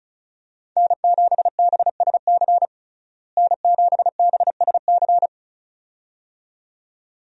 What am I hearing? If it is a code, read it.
Morse code, heard as a keyed tone, about 34 wpm, 700 Hz. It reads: N7BSC N7BSC
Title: USA Call Signs - 5 Characters - Repeated 2x - 35wpm